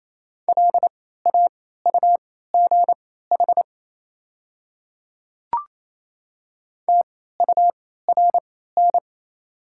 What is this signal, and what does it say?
Morse code, heard as a keyed tone, about 28 wpm, spaced as for 16 wpm, 700 Hz. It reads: LAUGH TURN
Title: 3x Character Spacing - Top 500 Words - 28wpm